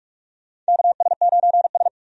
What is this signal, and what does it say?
Morse code, heard as a keyed tone, about 44 wpm, 700 Hz. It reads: KS9S
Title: USA Call Signs - 4 Characters - 45wpm